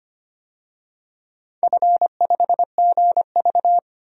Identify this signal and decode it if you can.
Morse code, heard as a keyed tone, about 25 wpm, 700 Hz. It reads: F5GV
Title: DXCC-100 Call Signs - 4 Characters - 25wpm